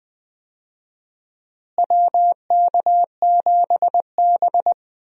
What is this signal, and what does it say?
Morse code, heard as a keyed tone, about 20 wpm, 700 Hz. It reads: WK7B
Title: USA Call Signs - 4 Characters - Repeated 3x - 20wpm